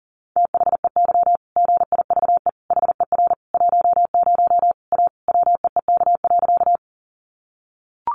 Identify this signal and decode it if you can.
Morse code, heard as a keyed tone, about 40 wpm, 700 Hz. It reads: THEY GIVE HER 10 A WEEK.
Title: Easy Phrases 40wpm